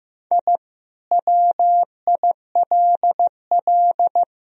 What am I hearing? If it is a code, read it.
Morse code, heard as a keyed tone, about 15 wpm, 700 Hz. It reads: I WILL
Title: Sentences from Top 1000 Words - Review Part 2 - 15wpm